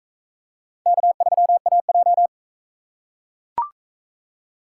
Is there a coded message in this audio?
Yes: Morse code from 0.9 to 2.3 s, heard as a keyed tone, about 41 wpm, 700 Hz. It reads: K3AJ